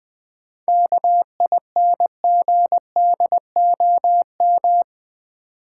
Morse code, heard as a keyed tone, about 20 wpm, 700 Hz. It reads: KINGDOM
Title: 7 Letter Words - First 3 Letters Given - 20wpm